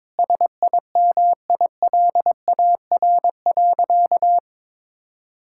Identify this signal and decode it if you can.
Morse code, heard as a keyed tone, about 21 wpm, 700 Hz. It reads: SIMILAR.
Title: Sentences from Top 1000 Words - Review Part 5 - 22wpm